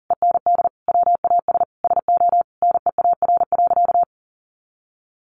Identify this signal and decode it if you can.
Morse code, heard as a keyed tone, about 37 wpm, 700 Hz. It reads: END WAS SO NEAR.
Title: Sentences from Top 300 Words 40wpm